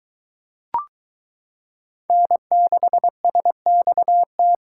Morse code, heard as a keyed tone, about 23 wpm, 700 Hz. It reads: N6SXT